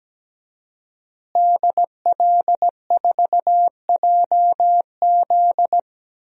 Morse code, heard as a keyed tone, about 17 wpm, 700 Hz. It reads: DL4JZ
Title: DXCC-100 Call Signs - 5 Characters - Repeated 3x - 17wpm